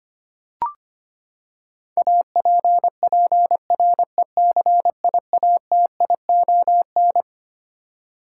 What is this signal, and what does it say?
Morse code, heard as a keyed tone, about 25 wpm, 700 Hz. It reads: APPRECIATION
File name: Twelve Letter Words - 25wpm